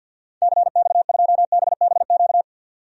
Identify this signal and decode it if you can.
Morse code, heard as a keyed tone, about 48 wpm, 700 Hz. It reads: CX2BBQ
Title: DXCC-100 Call Signs - 6 Characters - 50wpm